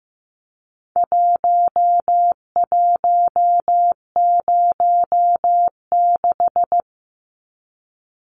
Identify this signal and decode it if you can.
Morse code, heard as a keyed tone, about 15 wpm, 700 Hz. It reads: 1106
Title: Numbers - Contest Style - 4-Digit 15wpm